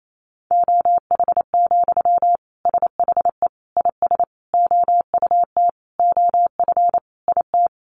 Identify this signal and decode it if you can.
Morse code, heard as a keyed tone, about 28 wpm, 700 Hz. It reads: OH, SHE IS OUT OF IT
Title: Sentences from Top 1100 Words 28wpm